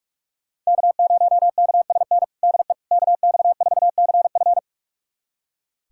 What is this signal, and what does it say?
Morse code, heard as a keyed tone, about 45 wpm, 700 Hz. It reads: K0KSN DE KX4CF
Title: USA Callsign DE Callsign 45wpm